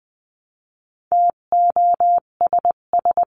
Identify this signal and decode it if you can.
Morse code, heard as a keyed tone, about 20 wpm, 700 Hz. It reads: TOSS
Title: 258 Lesson 42c - Words 16-20wpm